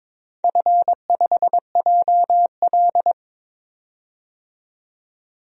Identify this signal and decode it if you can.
Morse code, heard as a keyed tone, about 23 wpm, 700 Hz. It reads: F5JL
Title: DXCC-100 Call Signs - 4 Characters - 22wpm